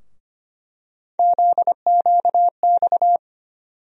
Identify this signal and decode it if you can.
Morse code, heard as a keyed tone, about 25 wpm, 700 Hz. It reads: ZQX
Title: USA Call Sign Postfixes - 3 Letters 25wpm